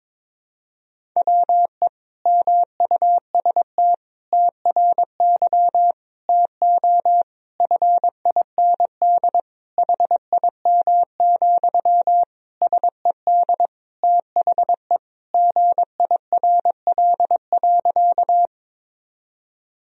Morse code, heard as a keyed tone, about 22 wpm, 700 Hz. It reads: WE MUST TRY TO FIND HIM, SED THE GIRL.
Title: Sentences from Top 400 Words and Abbreviations 22wpm